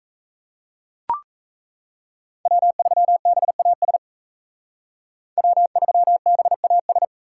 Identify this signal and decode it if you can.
Morse code, heard as a keyed tone, about 34 wpm, 700 Hz. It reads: W3BAS W3BAS
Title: Speed Racing - USA 1x3 Call Signs 28wpm